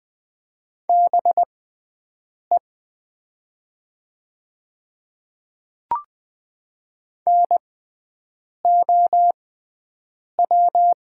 Morse code, heard as a keyed tone, about 20 wpm, spaced as for 7 wpm, 700 Hz. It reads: BE NOW
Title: 6x Character Spacing - Top 100 Words - 20wpm